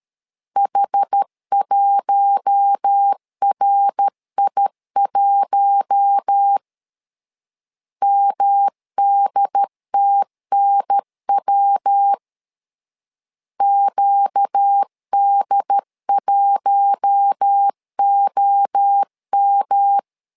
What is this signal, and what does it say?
Morse code, heard as a keyed tone, about 13 wpm, 800 Hz. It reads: H1RI1 MDTNW QD1OM
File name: LEZIONI / 4 Q1 12wpm 2028